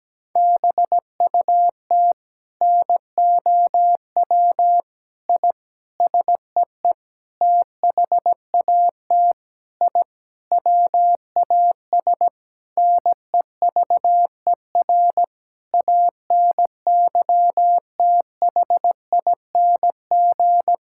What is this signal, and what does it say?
Morse code, heard as a keyed tone, about 17 wpm, 700 Hz. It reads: BUT NOW I SEE THAT I WAS NEVER ANYTHING
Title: Sentences from Top 1000 Words - Review Part 6 - 17wpm